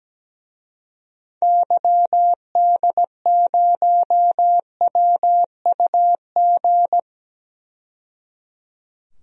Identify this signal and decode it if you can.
Morse code, heard as a keyed tone, about 17 wpm, 700 Hz. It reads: YD0WUG
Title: DXCC-100 Call Signs - 6 Characters - 17wpm